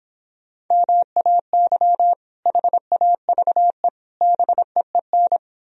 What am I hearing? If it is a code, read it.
Morse code, heard as a keyed tone, about 26 wpm, 700 Hz. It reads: MAY HAVE BEEN